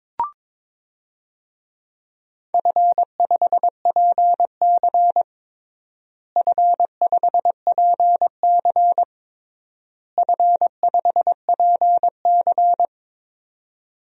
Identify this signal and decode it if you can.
Morse code, heard as a keyed tone, about 21 wpm, 700 Hz. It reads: F5PC F5PC F5PC
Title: DXCC-100 Call Signs - 4 Characters - Repeated 3x - 22wpm